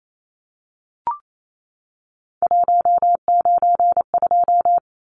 Morse code, heard as a keyed tone, about 28 wpm, 700 Hz. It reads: 192
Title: Numbers - 3-Digit 28wpm